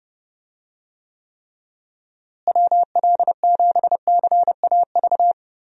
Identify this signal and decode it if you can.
Morse code, heard as a keyed tone, about 31 wpm, 700 Hz. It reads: WL7CAV
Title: USA Call Signs - 6 Characters - Repeated 3x - 30wpm